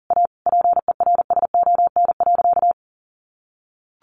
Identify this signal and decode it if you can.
Morse code, heard as a keyed tone, about 37 wpm, 700 Hz. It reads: A PERSON.
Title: Sentences from Top 1500 Words 40wpm